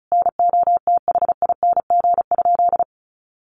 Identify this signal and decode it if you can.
Morse code, heard as a keyed tone, about 34 wpm, 700 Hz. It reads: NOTHING?
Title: Sentences from Top 500 Words 35wpm